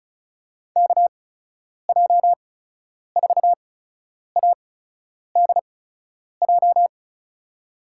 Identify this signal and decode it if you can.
Morse code, heard as a keyed tone, about 36 wpm, spaced as for 10 wpm, 700 Hz. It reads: KJ4ADJ
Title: USA Call Signs - 6 Characters - 8x Character Spacing - 35wpm